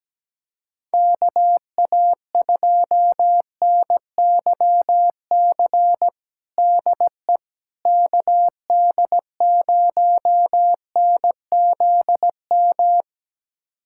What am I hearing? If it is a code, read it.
Morse code, heard as a keyed tone, about 17 wpm, 700 Hz. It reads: KA2NYC DE KD0NZM
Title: USA Callsign DE Callsign - 6 Characters - 17wpm